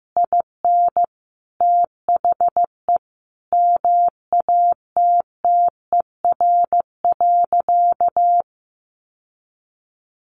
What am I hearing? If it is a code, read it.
Morse code, heard as a keyed tone, about 14 wpm, 700 Hz. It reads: IN THE MATTER.
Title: Sentences from Top 1200 Words 15wpm